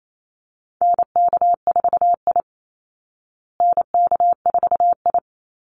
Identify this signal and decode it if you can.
Morse code, heard as a keyed tone, about 28 wpm, 700 Hz. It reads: NK4I NK4I
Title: Call Signs Repeated 2x - 28wpm